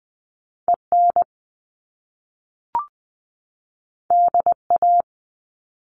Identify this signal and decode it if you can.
Morse code, heard as a keyed tone, about 20 wpm, 700 Hz. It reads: EN DA
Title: Top Two Letter Combinations in Words 20wpm